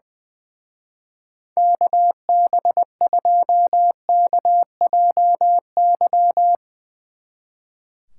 Morse code, heard as a keyed tone, about 20 wpm, 700 Hz. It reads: KB2KJY
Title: Speed Racing - USA 2x3 Call Signs 20wpm